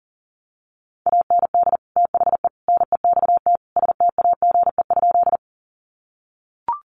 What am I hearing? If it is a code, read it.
Morse code, heard as a keyed tone, about 40 wpm, 700 Hz. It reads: AND THE NEXT STAGE?
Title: Sentences from Top 1200 Words 40wpm